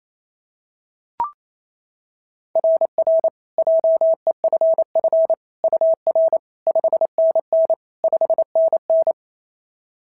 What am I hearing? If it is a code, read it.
Morse code, heard as a keyed tone, about 25 wpm, 650 Hz. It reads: RR JEFF UR 5NN 5NN
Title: SOTA QSO Practice 28wpm